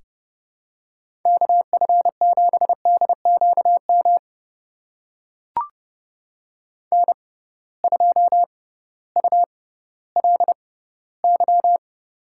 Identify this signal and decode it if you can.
Morse code, heard as a keyed tone, about 31 wpm, spaced as for 16 wpm, 700 Hz. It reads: KF7DQM N2ULY
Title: USA Call Signs - 6x Character Spacing - 30wpm